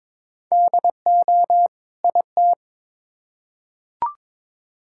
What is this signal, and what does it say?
Morse code, heard as a keyed tone, about 22 wpm, 700 Hz. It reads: DO IT
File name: Build A Phrase - 2-Word Phrases - 22wpm